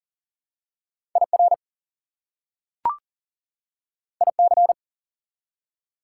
Morse code, heard as a keyed tone, about 40 wpm, 700 Hz. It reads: IR IC
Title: Prefixes and Suffixes 40wpm